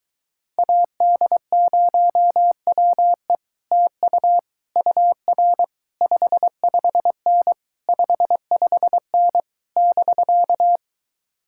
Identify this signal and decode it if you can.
Morse code, heard as a keyed tone, about 23 wpm, 700 Hz. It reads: AD0WE TU UR 55N 55N <BK>